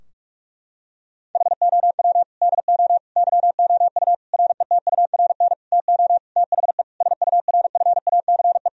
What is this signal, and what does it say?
Morse code, heard as a keyed tone, about 41 wpm, 700 Hz. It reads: HOW DO YOU RETURN TO THE SURFACE
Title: Sentences from Top 1000 Words - Review Part 3 - 45wpm